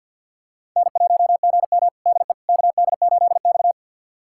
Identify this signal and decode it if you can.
Morse code, heard as a keyed tone, about 53 wpm, 700 Hz. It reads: N1GM DE KD8X
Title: USA Callsign DE Callsign - 4 Characters - 50wpm